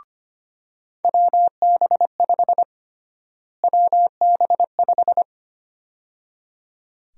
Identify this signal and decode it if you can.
Morse code, heard as a keyed tone, about 25 wpm, 700 Hz. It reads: WB5 WB5
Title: USA Call Sign Prefixes - Repeated 2x - 25wpm